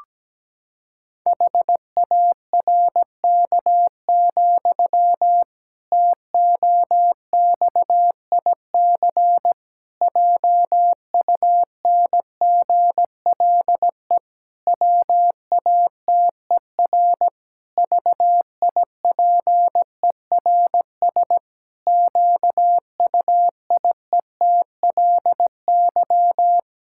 Morse code, heard as a keyed tone, about 17 wpm, 700 Hz. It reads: HARK, TOXIC JUNGLE WATER VIPERS QUIETLY
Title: Pangram Challenge v2 - 17wpm